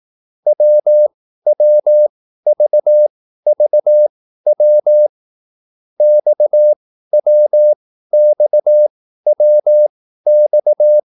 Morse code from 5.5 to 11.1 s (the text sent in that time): XWXWX